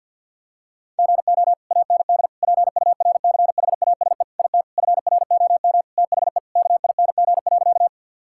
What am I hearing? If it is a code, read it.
Morse code, heard as a keyed tone, about 50 wpm, 700 Hz. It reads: GO AND PURCHASE IT FROM THE KING.
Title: Sentences from Top 1000 Words - Review Part 5 - 50wpm